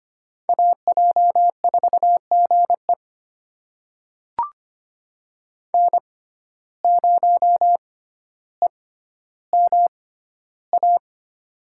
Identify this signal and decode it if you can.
Morse code, heard as a keyed tone, about 25 wpm, spaced as for 13 wpm, 690 Hz. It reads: AJ4GE N0EMA
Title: USA Call Signs - 5 Characters - 6x Character Spacing - 25wpm